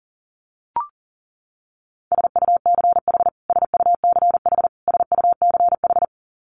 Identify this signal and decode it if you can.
Morse code, heard as a keyed tone, about 37 wpm, 700 Hz. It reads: SUCH SUCH SUCH
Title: Top 500 Words Repeated 40wpm